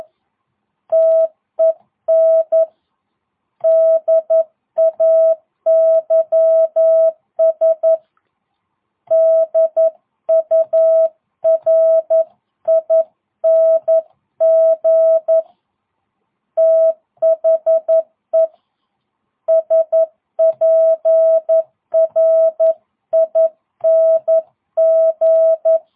Morse code, heard as a keyed tone, about 11 wpm, 650 Hz. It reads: TEN DAYS DURING THE SPRING